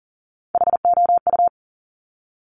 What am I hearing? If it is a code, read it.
Morse code, heard as a keyed tone, about 38 wpm, 690 Hz. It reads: HOU
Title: Top Three Letter Combinations in Words 40wpm